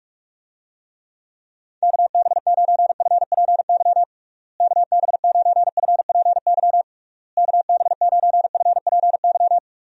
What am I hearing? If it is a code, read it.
Morse code, heard as a keyed tone, about 41 wpm, 700 Hz. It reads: KB9FPY KB9FPY KB9FPY
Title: USA Call Signs - 6 Characters - Repeated 3x - 45wpm